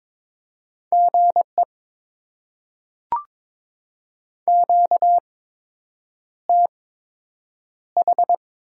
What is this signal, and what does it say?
Morse code, heard as a keyed tone, about 22 wpm, spaced as for 8 wpm, 700 Hz. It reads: GE QTH
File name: Top 20 QSO Elements - 8x Character Spacing - 22wpm